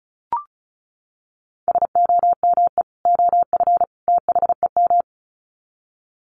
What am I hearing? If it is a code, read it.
Morse code, heard as a keyed tone, about 34 wpm, 700 Hz. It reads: SOME OF THEM
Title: Sets of 3 Words 35wpm